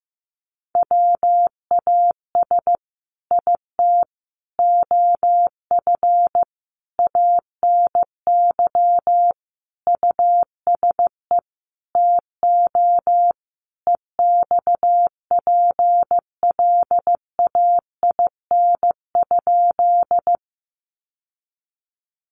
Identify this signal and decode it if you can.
Morse code, heard as a keyed tone, about 15 wpm, 700 Hz. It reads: WAS IT OF ANY USE TO EXPLAIN?
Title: Sentences from Top 1100 Words 15wpm